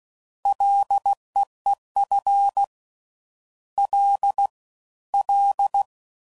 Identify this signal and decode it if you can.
Morse code, heard as a keyed tone, about 16 wpm, 800 Hz. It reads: LEEF L L
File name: Prefixes / FLs & THs / Single Fs Ls